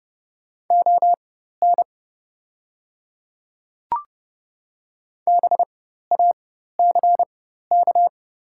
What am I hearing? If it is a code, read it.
Morse code, heard as a keyed tone, about 30 wpm, spaced as for 14 wpm, 700 Hz. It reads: ON BACK
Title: 4x Character Spacing - Top 300 Words - 30wpm